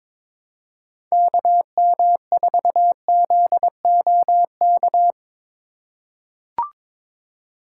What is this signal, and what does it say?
Morse code, heard as a keyed tone, about 22 wpm, 700 Hz. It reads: KM4ZOK
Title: USA Call Signs - 6 Characters - 3x Character Spacing - 22wpm